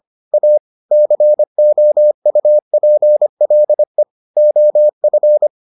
Morse code, heard as a keyed tone, about 25 wpm, 600 Hz. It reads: A COUPLE OF